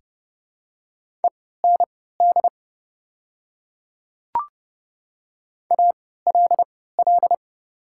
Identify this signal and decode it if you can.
Morse code, heard as a keyed tone, about 30 wpm, spaced as for 17 wpm, 700 Hz. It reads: END ALL
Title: 3x Character Spacing - Top 200 Words - 30wpm